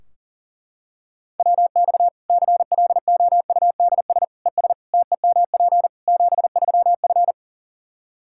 Morse code, heard as a keyed tone, about 40 wpm, 700 Hz. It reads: WX CLOUDS ES TEMP 73F
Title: Practice QSO - Exchange 2 - Info - 40wpm